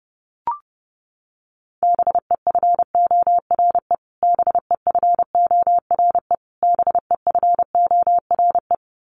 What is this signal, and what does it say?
Morse code, heard as a keyed tone, about 30 wpm, 700 Hz. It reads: BEFORE BEFORE BEFORE
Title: Top 300 Words Repeated 30wpm